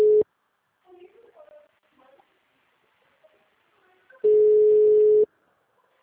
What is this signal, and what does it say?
Ringback tone (Europe, 425 Hz).
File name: out- -2004- - - .4